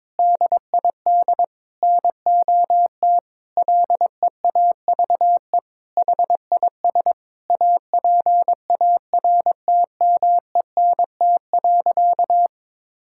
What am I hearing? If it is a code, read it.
Morse code, heard as a keyed tone, about 23 wpm, 700 Hz. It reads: DID NOT LEAVE HIS APARTMENT.
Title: Sentences from Top 1000 Words - Review Part 6 - 22wpm